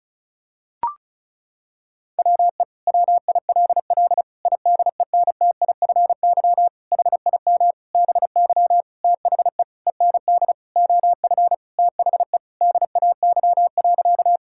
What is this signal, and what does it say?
Morse code, heard as a keyed tone, about 36 wpm, 700 Hz. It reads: WE WILL IDENTIFY HIM BY THE END OF THE DAY.
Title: Sentences from Top 1000 Words - Review Part 6 - 35wpm